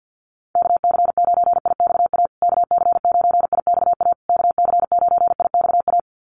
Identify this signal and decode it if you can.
Morse code, heard as a keyed tone, about 50 wpm, 700 Hz. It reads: KC9IXA KC9IXA KC9IXA
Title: USA Call Signs Repeated 3x 50wpm